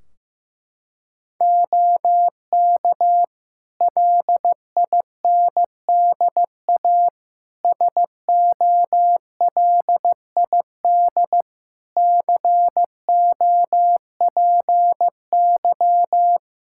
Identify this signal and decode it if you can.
Morse code, heard as a keyed tone, about 15 wpm, 700 Hz. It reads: OK LINDA SOLID COPY
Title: Practice QSO - Exchange 3 - More Info - 15wpm